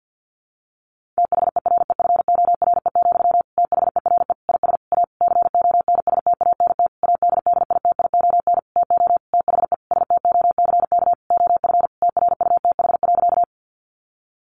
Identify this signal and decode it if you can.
Morse code, heard as a keyed tone, about 50 wpm, 700 Hz. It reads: THEREFORE, THERE IS A CONSTANT ADDITION TO THE STOCK OF TRUTH.
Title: Sentences from Top 1500 Words 50wpm